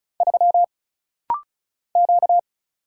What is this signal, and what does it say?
Morse code, heard as a keyed tone, about 36 wpm, 700 Hz. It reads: 3 Q
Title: Single Letter-Number - Rapid-Fire 35wpm